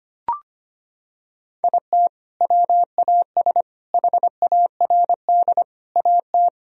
Morse code, heard as a keyed tone, about 25 wpm, 700 Hz. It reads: IT WAS HARD AT
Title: Sentences from Top 1000 Words - Review Part 1 - 25wpm